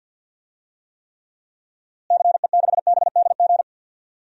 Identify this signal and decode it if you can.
Morse code, heard as a keyed tone, about 52 wpm, 700 Hz. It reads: CE6BDG
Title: DXCC-100 Call Signs - 6 Characters - Repeated 3x - 50wpm